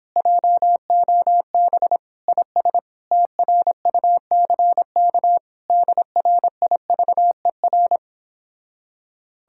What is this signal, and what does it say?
Morse code, heard as a keyed tone, about 29 wpm, 700 Hz. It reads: JOB IS TRUCK DRIVER